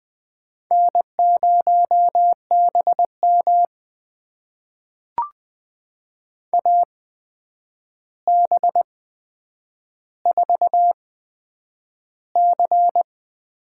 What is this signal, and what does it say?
Morse code, heard as a keyed tone, about 21 wpm, spaced as for 9 wpm, 700 Hz. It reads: N0BM AB4C
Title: USA Call Signs - 4 Characters - 8x Character Spacing - 20wpm